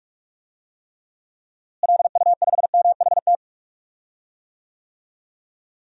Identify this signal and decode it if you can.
Morse code, heard as a keyed tone, about 46 wpm, 700 Hz. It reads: LU5MHT
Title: DXCC-100 Call Signs - 6 Characters - Repeated 2x - 45wpm